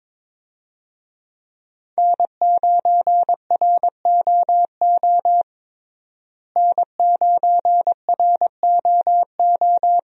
Morse code, heard as a keyed tone, about 22 wpm, 700 Hz. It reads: N9ROO N9ROO
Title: USA Call Signs - 5 Characters - Repeated 3x - 22wpm